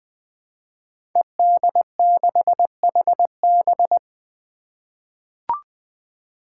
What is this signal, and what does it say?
Morse code, heard as a keyed tone, about 20 wpm, 700 Hz. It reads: ED6HB
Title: Five Challenging Letter-Number Combinations 20wpm